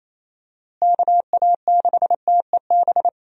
Morse code, heard as a keyed tone, about 28 wpm, 700 Hz. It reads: KA6TEB
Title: USA Call Signs - 6 Characters - Repeated 3x - 28wpm